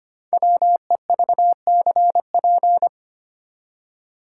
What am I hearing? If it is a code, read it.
Morse code, heard as a keyed tone, about 25 wpm, 690 Hz. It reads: WEVCP
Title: Code Groups - 5 Letter:Numbers - 25wpm